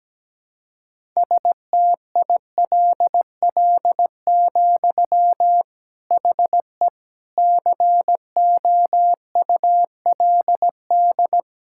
Morse code, heard as a keyed tone, about 17 wpm, 700 Hz. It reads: STILL, HE COULD
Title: Sentences from Top 1000 Words - Review Part 5 - 17wpm